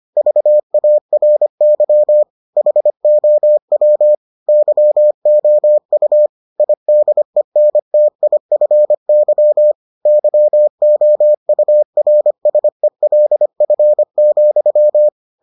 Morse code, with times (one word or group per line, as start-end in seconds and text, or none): 0.2-2.2 VARY
2.6-4.1 HOW
4.5-6.3 YOU
6.6-9.7 IDENTIFY
10.1-15.1 YOURSELF,